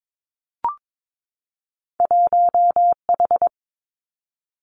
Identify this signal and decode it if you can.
Morse code, heard as a keyed tone, about 22 wpm, 700 Hz. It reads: 1H
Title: Two Challenging Letter-Number Combinations 22wpm